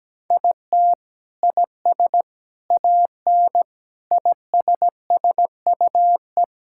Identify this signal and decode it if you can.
Morse code, heard as a keyed tone, about 17 wpm, 700 Hz. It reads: IT IS AN ISSUE
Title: Sentences from Top 1000 Words - Review Part 6 - 17wpm